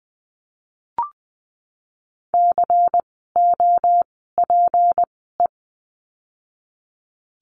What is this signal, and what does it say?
Morse code, heard as a keyed tone, about 20 wpm, spaced as for 14 wpm, 700 Hz. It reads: COPE
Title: 153 Lesson 25f - Letter p - Words 20wpm